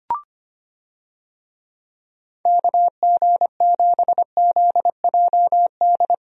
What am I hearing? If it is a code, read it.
Morse code, heard as a keyed tone, about 25 wpm, 700 Hz. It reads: KG7ZJD
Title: USA Call Signs - 6 Characters - Repeated 3x - 25wpm